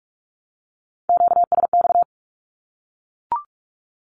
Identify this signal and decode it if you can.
Morse code, heard as a keyed tone, about 44 wpm, 700 Hz. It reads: QSX
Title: Q-Codes Repeated 3x 45wpm